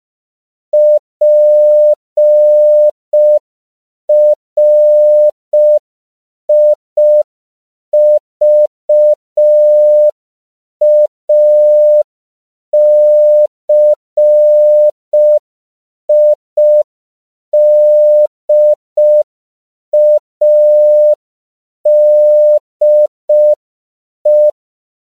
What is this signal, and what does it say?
Morse code, heard as a keyed tone, about 5 wpm, 600 Hz. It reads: PRIVACIDADE
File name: CW / 5PPM